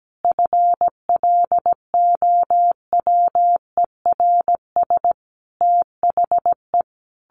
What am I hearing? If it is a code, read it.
Morse code, heard as a keyed tone, about 17 wpm, 700 Hz. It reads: FLOWERS THE
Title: A Sound Of Thunder 17wpm